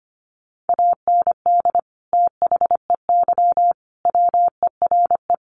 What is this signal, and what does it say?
Morse code, heard as a keyed tone, about 25 wpm, 700 Hz. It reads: AND THEY WERE
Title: Sentences from Top 1500 Words 25wpm